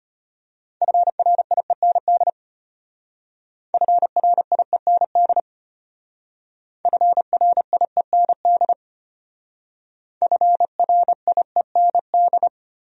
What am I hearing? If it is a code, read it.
Morse code, heard as a keyed tone, about 32 wpm, 700 Hz. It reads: FRIEND FRIEND FRIEND FRIEND